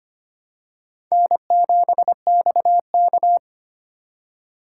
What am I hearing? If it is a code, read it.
Morse code, heard as a keyed tone, about 25 wpm, 700 Hz. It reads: N7XK
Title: USA Call Signs - 4 Characters - 4x Character Spacing - 25wpm